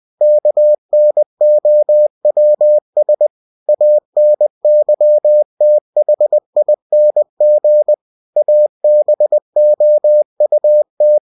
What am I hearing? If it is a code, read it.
Morse code, heard as a keyed tone, about 20 wpm, 600 Hz. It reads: KNOWS ANYTHING ABOUT
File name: cw-books / winnie-the-pooh / winnie-20wpm-page-0028